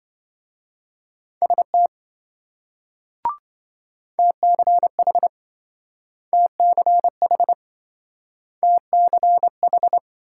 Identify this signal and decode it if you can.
Morse code, heard as a keyed tone, about 27 wpm, 700 Hz. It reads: ST TCH TCH TCH